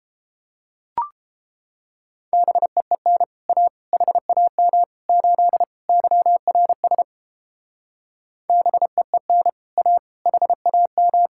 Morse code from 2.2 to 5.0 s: BEEN A HAM

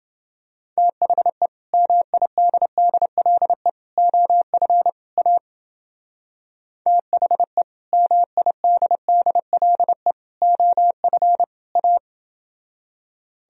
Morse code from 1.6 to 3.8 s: MIDDLE

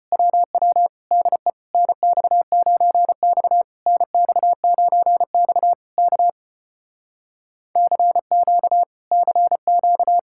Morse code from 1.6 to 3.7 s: NX9X

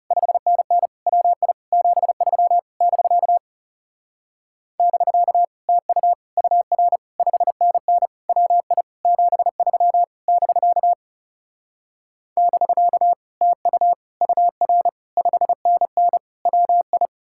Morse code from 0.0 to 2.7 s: 5NN WI 73

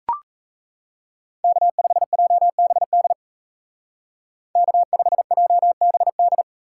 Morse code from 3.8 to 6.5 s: K5JBD

